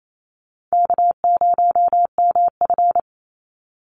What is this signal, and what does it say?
Morse code, heard as a keyed tone, about 28 wpm, 700 Hz. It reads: K0MF
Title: USA Call Signs 28wpm